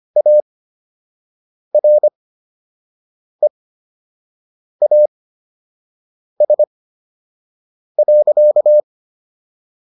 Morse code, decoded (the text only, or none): AREAS.